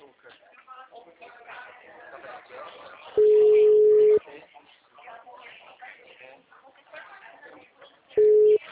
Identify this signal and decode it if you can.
Ringback tone (Europe, 425 Hz), with speech.